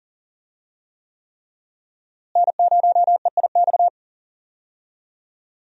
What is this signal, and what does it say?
Morse code, heard as a keyed tone, about 37 wpm, 700 Hz. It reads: N0EIX
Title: USA Call Signs - 5 Characters - Repeated 3x - 40wpm